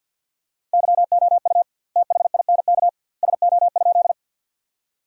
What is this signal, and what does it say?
Morse code, heard as a keyed tone, about 50 wpm, 700 Hz. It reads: YOU THINK SO?
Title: 2x Word-Spacing - Sentences from Top 200 Words - 50wpm